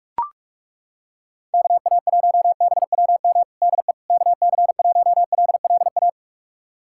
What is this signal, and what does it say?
Morse code, heard as a keyed tone, about 42 wpm, 700 Hz. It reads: KA1BWM DE KC1LLA
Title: USA Callsign DE Callsign 45wpm